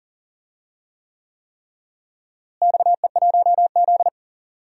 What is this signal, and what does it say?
Morse code, heard as a keyed tone, about 40 wpm, 700 Hz. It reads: XE1Z
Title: DXCC-100 Call Signs - 4 Characters - Repeated 3x - 40wpm